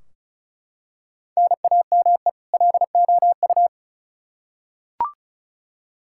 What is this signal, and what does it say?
Morse code, heard as a keyed tone, about 34 wpm, 700 Hz. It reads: NAME LOU
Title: Practice QSO - Exchange 1 - Beginning - 35wpm